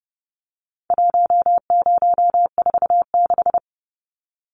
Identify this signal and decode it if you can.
Morse code, heard as a keyed tone, about 30 wpm, 700 Hz. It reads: 1046
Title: Numbers - 4-Digit 30wpm